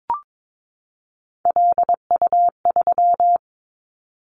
Morse code, heard as a keyed tone, about 22 wpm, 700 Hz. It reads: LU3
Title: Callsign Prefixes for Top 100 DXCC - 22wpm